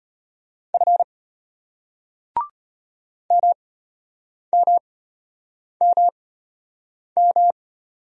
Morse code, heard as a keyed tone, about 33 wpm, spaced as for 8 wpm, 700 Hz. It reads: F MMMM